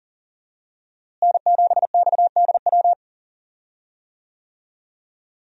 Morse code, heard as a keyed tone, about 40 wpm, 700 Hz. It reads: N7XDW
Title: USA Call Signs - 5 Characters - Repeated 3x - 40wpm